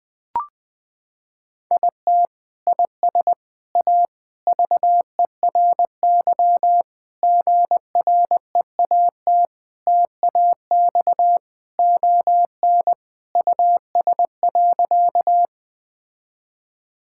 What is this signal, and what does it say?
Morse code, heard as a keyed tone, about 21 wpm, 700 Hz. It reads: IT IS A VERY GREAT TAX ON US.
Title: Sentences from Top 1000 Words - Review Part 5 - 20wpm